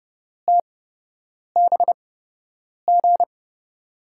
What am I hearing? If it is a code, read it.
Morse code, heard as a keyed tone, about 30 wpm, spaced as for 8 wpm, 700 Hz. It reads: TBG
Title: USA Call Signs - 6 Characters - 8x Character Spacing - 30wpm